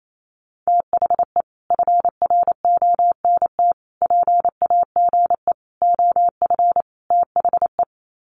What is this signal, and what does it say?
Morse code, heard as a keyed tone, about 28 wpm, 700 Hz. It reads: THE FRONT PAGE OF THE
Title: Sets of 5 Words Encore 28wpm